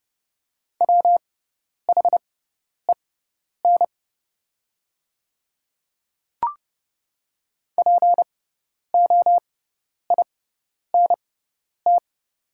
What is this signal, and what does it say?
Morse code, heard as a keyed tone, about 30 wpm, spaced as for 10 wpm, 700 Hz. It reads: WHEN POINT
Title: 6x Character Spacing - Top 300 Words - 30wpm